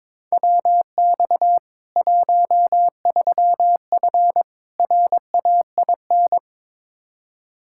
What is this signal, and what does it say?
Morse code, heard as a keyed tone, about 22 wpm, 700 Hz. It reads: WX 13F RAIN
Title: Practice QSO - Exchange 2 - Info - 22wpm